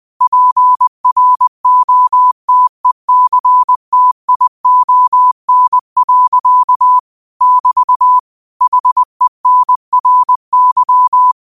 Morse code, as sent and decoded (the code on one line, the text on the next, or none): .--. .-. --- - . -.-. - .. --- -. .-.-.- / -...- / .... -. .-. -.--
PROTECTION. <BT> HENRY